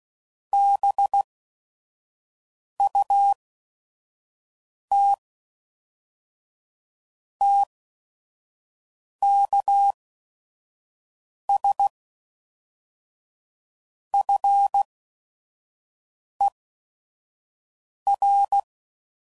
Morse code, heard as a keyed tone, about 16 wpm, spaced as for 5 wpm, 800 Hz. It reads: BUT TKS FER
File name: CW Code Groups @5WPM / QSOs / kc7sac